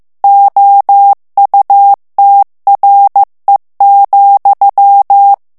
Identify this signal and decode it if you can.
Morse code, heard as a keyed tone, about 15 wpm, 800 Hz. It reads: OUTRE,
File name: morse mp3 / FRANCAIS TEXTE CLAIR 1 VITESSE 14 MOTS MINUTES